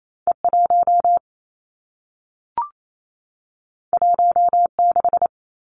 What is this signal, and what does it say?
Morse code, heard as a keyed tone, about 28 wpm, 700 Hz. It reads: E 1 1 6
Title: Two Challenging Letter-Number Combinations 28wpm